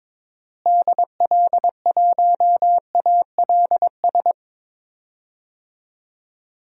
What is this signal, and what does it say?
Morse code, heard as a keyed tone, about 22 wpm, 700 Hz. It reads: DL1ALS
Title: DXCC-100 Call Signs - 6 Characters - Repeated 3x - 22wpm